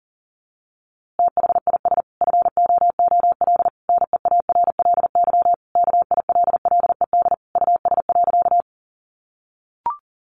Morse code, heard as a keyed tone, about 40 wpm, 700 Hz. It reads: THIS FOOL NEARLY KILLED US.
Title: A Sound Of Thunder 40wpm